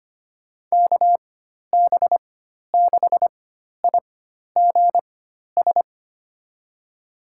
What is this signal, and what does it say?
Morse code, heard as a keyed tone, about 25 wpm, spaced as for 12 wpm, 700 Hz. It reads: KB6IGS
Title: USA Call Signs - 6 Characters - 4x Character Spacing - 25wpm